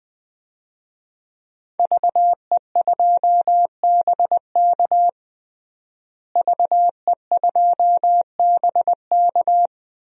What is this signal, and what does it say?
Morse code, heard as a keyed tone, about 20 wpm, 700 Hz. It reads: VE2BK VE2BK
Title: DXCC-100 Call Signs - 5 Characters - Repeated 3x - 20wpm